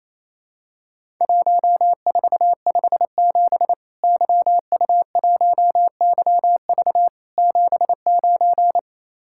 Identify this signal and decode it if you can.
Morse code, heard as a keyed tone, about 28 wpm, 700 Hz. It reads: 1457 YU1YV 79
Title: Worked All Europe DX Contest - QTC Exchange - 28wpm